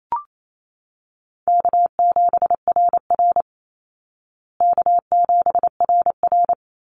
Morse code, heard as a keyed tone, about 28 wpm, 700 Hz. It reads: K7RR K7RR
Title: Easy Call Signs Repeated 2x - 28wpm